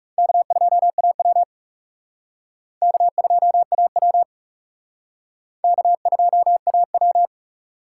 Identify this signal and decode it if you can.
Morse code, heard as a keyed tone, about 37 wpm, 700 Hz. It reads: K2AW K2AW K2AW
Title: Speed Racing - USA Call Signs 30wpm